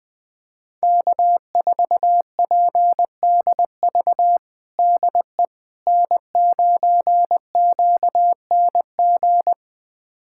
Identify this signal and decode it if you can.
Morse code, heard as a keyed tone, about 21 wpm, 700 Hz. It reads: K4PDV DE N9QNG
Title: USA Callsign DE Callsign 20wpm